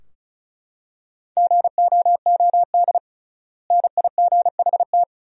Morse code, heard as a keyed tone, about 36 wpm, 700 Hz. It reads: GOOD NIGHT
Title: 3x Word-Spacing - Sentences from Top 300 Words - 35wpm